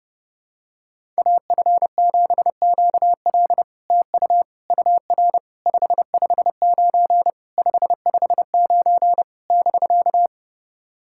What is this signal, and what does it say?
Morse code, heard as a keyed tone, about 32 wpm, 700 Hz. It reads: AF7QL TU UR 559 559 <BK>
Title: SOTA QSO Practice 30wpm